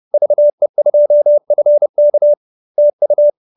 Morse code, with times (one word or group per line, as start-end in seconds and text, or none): 0.1-2.3 VE2FK
2.8-3.3 TU